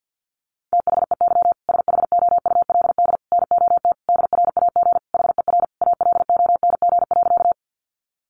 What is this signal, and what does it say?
Morse code, heard as a keyed tone, about 50 wpm, 700 Hz. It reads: THEY SHOULD NOT DRAG HER ALONG.
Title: Sentences from Top 1500 Words 50wpm